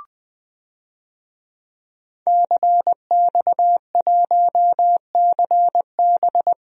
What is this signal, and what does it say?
Morse code, heard as a keyed tone, about 20 wpm, 700 Hz. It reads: CX1CB
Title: DXCC-100 Call Signs - 5 Characters - Repeated 3x - 20wpm